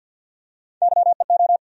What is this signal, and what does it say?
Morse code, heard as a keyed tone, about 53 wpm, 700 Hz. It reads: YEO
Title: USA Call Sign Postfixes - 3 Letters 50wpm